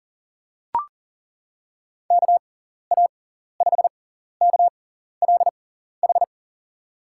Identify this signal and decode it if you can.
Morse code, heard as a keyed tone, about 40 wpm, spaced as for 14 wpm, 700 Hz. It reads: KA5KLH
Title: USA Call Signs - 6 Characters - 6x Character Spacing - 40wpm